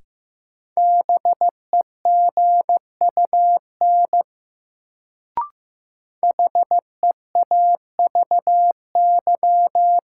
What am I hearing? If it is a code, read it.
Morse code, heard as a keyed tone, about 15 wpm, 700 Hz. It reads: BEGUN HEAVY
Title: Sentences from Top 1000 Words - Review Part 6 - New Word Drill - 15wpm